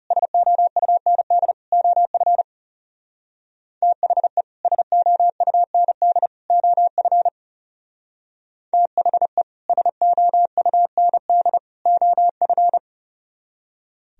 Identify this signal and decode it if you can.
Morse code, heard as a keyed tone, about 38 wpm, 700 Hz. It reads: SOUND OF THE SOUND OF THE SOUND OF